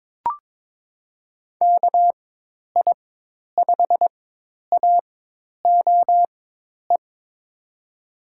Morse code, heard as a keyed tone, about 22 wpm, spaced as for 10 wpm, 700 Hz. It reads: KI5AOE
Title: USA Call Signs - 4x Character Spacing - 22wpm